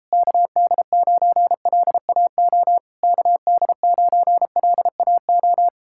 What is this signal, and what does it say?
Morse code, heard as a keyed tone, about 37 wpm, 700 Hz. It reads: KD9LAO KD9LAO